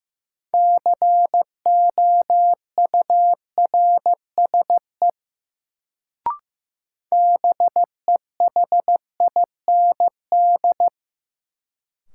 Morse code, heard as a keyed tone, about 15 wpm, 700 Hz. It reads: COURSE BEHIND
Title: Sentences from Top 500 Words - New Word Drill - 15wpm